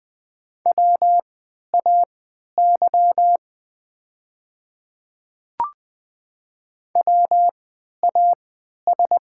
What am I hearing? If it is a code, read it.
Morse code, heard as a keyed tone, about 20 wpm, spaced as for 11 wpm, 700 Hz. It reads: WAY WAS
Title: 3x Character Spacing - Top 100 Words - 20wpm